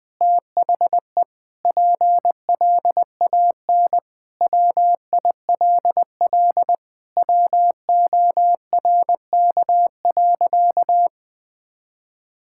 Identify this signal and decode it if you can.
Morse code, heard as a keyed tone, about 20 wpm, 700 Hz. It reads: THE PLAN WILL WORK.
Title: Sentences with 8-Word Phrases - 20wpm